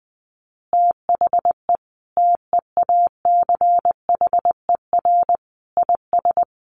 Morse code, heard as a keyed tone, about 19 wpm, 700 Hz. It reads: THE TEACHER IS
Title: Easy Phrases 20wpm